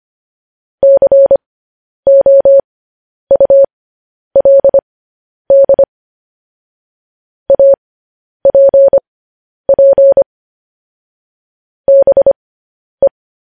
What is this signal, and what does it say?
Morse code, heard as a keyed tone, about 25 wpm, spaced as for 10 wpm, 550 Hz. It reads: COULD APP BE